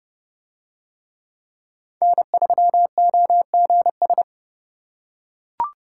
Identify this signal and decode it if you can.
Morse code, heard as a keyed tone, about 29 wpm, 700 Hz. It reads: N3OGS